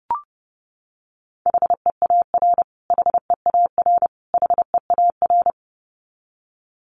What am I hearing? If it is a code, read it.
Morse code, heard as a keyed tone, about 30 wpm, 700 Hz. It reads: HEAR HEAR HEAR
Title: Top 300 Words Repeated 30wpm